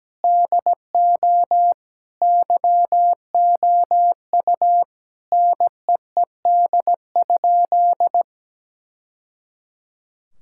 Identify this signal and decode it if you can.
Morse code, heard as a keyed tone, about 17 wpm, 700 Hz. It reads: DO YOU NEED?
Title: Sentences from Top 1000 Words - Review Part 5 - 17wpm